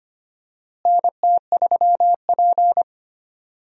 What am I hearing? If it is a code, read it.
Morse code, heard as a keyed tone, about 25 wpm, 700 Hz. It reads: NT3P
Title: USA Call Signs - 4 Characters - Repeated 3x - 25wpm